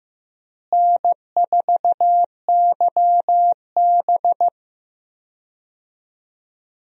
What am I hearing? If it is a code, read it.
Morse code, heard as a keyed tone, about 15 wpm, 700 Hz. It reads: N4YB
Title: USA Call Signs - 4 Characters - Repeated 3x - 15wpm